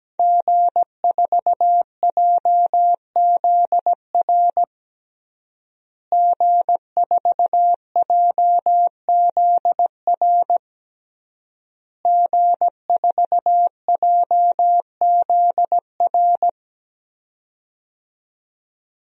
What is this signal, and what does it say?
Morse code, heard as a keyed tone, about 17 wpm, 700 Hz. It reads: G4JZR G4JZR G4JZR
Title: DXCC-100 Call Signs - 5 Characters - Repeated 3x - 17wpm